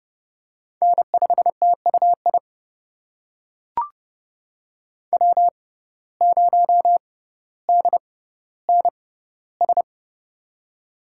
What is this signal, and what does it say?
Morse code, heard as a keyed tone, about 31 wpm, spaced as for 15 wpm, 700 Hz. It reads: N5TUI W0DNS
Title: USA Call Signs - 5 Characters - 6x Character Spacing - 30wpm